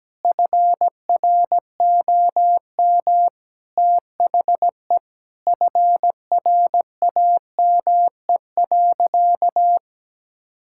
Morse code, heard as a keyed tone, about 17 wpm, 700 Hz. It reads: FROM THE FRAME.
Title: Sentences from Top 1000 Words - Review Part 6 - 17wpm